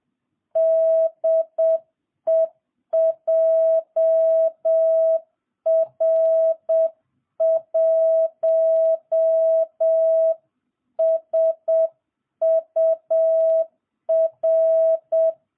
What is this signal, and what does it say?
Morse code, heard as a keyed tone, about 7 wpm, 650 Hz. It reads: DE JR1SUR